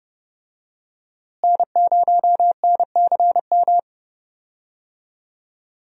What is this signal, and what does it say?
Morse code, heard as a keyed tone, about 31 wpm, 700 Hz. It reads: N0NCM
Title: USA Call Signs - 5 Characters - Repeated 3x - 30wpm